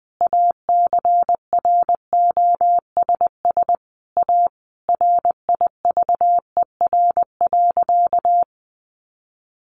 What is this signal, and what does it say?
Morse code, heard as a keyed tone, about 20 wpm, 700 Hz. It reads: ACROSS A RIVER.
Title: Sentences from Top 1200 Words 20wpm